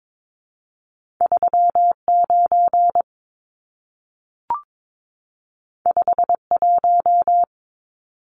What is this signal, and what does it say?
Morse code, heard as a keyed tone, about 22 wpm, 700 Hz. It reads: 39 51
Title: Numbers - 2-Digit 22wpm